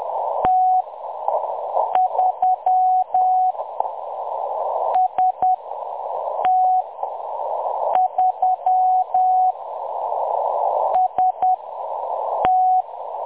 Morse code, heard as a keyed tone, about 10 wpm, spaced as for 6 wpm, 740 Hz. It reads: T3 ST3 ST
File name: M23-ST3- -0858utc-10184khz